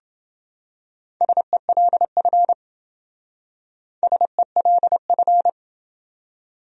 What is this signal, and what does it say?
Morse code, heard as a keyed tone, about 28 wpm, 700 Hz. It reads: SELF SELF